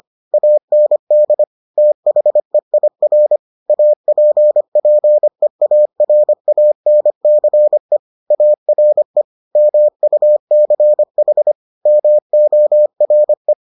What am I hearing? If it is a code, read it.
Morse code, heard as a keyed tone, about 25 wpm, 600 Hz. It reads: AND THEIR APPEARANCE ARE MUCH MORE